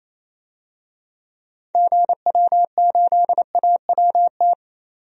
Morse code, heard as a keyed tone, about 28 wpm, 700 Hz. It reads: GW8AWT
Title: DXCC-100 Call Signs - 6 Characters - 28wpm